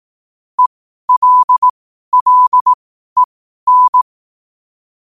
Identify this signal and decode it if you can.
Morse code, heard as a keyed tone, about 18 wpm, spaced as for 12 wpm, 1000 Hz. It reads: ELLEN
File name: qotdfiles / 12wpm